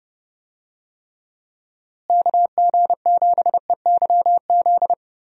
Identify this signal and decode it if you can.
Morse code, heard as a keyed tone, about 30 wpm, 700 Hz. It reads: KG7EYZ
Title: USA Call Signs - 6 Characters - Repeated 3x - 30wpm